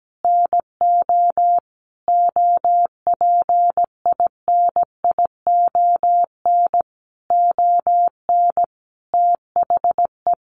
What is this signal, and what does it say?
Morse code, heard as a keyed tone, about 17 wpm, 700 Hz. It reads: NO OPINION ON THE
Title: Sentences from Top 1200 Words 17wpm